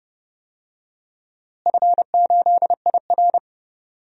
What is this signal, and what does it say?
Morse code, heard as a keyed tone, about 30 wpm, 700 Hz. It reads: F8IR
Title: DXCC-100 Call Signs - 4 Characters - Repeated 2x - 30wpm